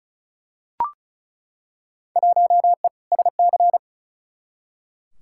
Morse code, heard as a keyed tone, about 36 wpm, 700 Hz. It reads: 1E SC
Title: Field Day Exchanges - 35wpm